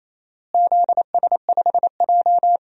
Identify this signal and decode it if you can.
Morse code, heard as a keyed tone, about 27 wpm, 700 Hz. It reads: ZS5J
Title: DXCC-100 Call Signs - 4 Characters - Repeated 3x - 28wpm